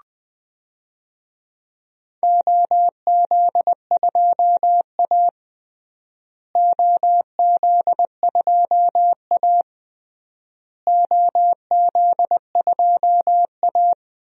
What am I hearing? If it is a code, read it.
Morse code, heard as a keyed tone, about 20 wpm, 700 Hz. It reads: OZ2A OZ2A OZ2A
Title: DXCC-100 Call Signs - 4 Characters - Repeated 3x - 20wpm